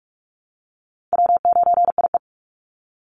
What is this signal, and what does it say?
Morse code, heard as a keyed tone, about 44 wpm, 700 Hz. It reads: W9IE
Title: Easy Call Signs - 45wpm